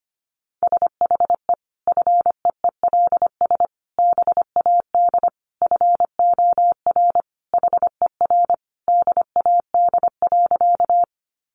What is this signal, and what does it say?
Morse code, heard as a keyed tone, about 25 wpm, 700 Hz. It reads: SHE FEELS BAD FOR HER DAD.
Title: Additional Easy Phrases 25wpm